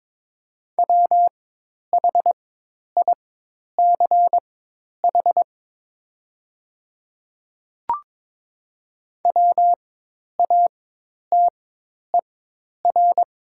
Morse code, heard as a keyed tone, about 21 wpm, spaced as for 10 wpm, 700 Hz. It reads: WHICH WATER
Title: 4x Character Spacing - Top 200 Words - 22wpm